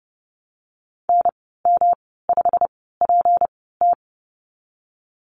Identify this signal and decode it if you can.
Morse code, heard as a keyed tone, about 30 wpm, spaced as for 17 wpm, 700 Hz. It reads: NM5PT
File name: Lesson 37g - Letter x - Call Signs 30wpm